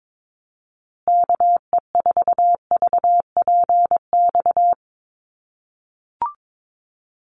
Morse code, heard as a keyed tone, about 22 wpm, 700 Hz. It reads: KE4VPX
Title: USA Call Signs Repeated 2x 22wpm